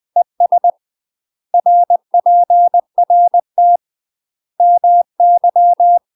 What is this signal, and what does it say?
Morse code, heard as a keyed tone, about 20 wpm, 700 Hz. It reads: ES RPRT MY
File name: UK MASTER QSOs TRAINING/20WPM